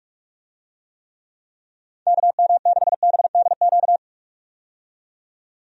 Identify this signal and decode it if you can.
Morse code, heard as a keyed tone, about 45 wpm, 700 Hz. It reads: KM6BDQ
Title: USA Call Signs - 6 Characters - 45wpm